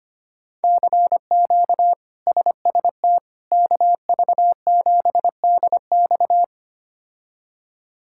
Morse code, heard as a keyed tone, about 25 wpm, 700 Hz. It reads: CQ SST KV7DX
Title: K1USN SST QSO Practice - 25wpm